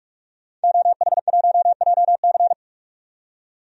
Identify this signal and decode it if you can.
Morse code, heard as a keyed tone, about 46 wpm, 700 Hz. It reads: OH1JC
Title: DXCC-100 Call Signs - 5 Characters - 45wpm